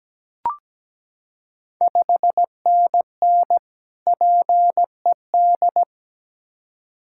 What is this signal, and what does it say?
Morse code, heard as a keyed tone, about 17 wpm, 700 Hz. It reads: 5NN PED
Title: Ontario QSO Party Exchange 17wpm